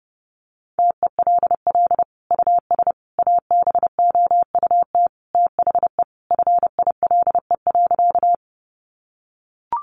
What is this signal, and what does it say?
Morse code, heard as a keyed tone, about 30 wpm, 700 Hz. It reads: TELL US ABOUT THE FILE.
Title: Sentences from Top 1200 Words 30wpm